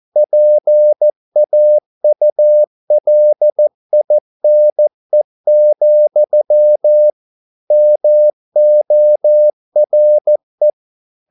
Morse code, heard as a keyed tone, about 14 wpm, 600 Hz. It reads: PAULINE, MORE